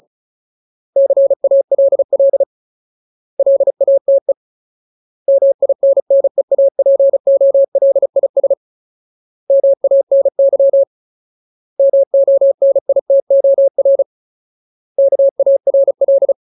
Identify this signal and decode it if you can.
Morse code, heard as a keyed tone, about 36 wpm, 550 Hz. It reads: CALL LATE MINNEAPOLIS MANY MONITOR KARL
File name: Ultimate Word List / 35WPM Ultimate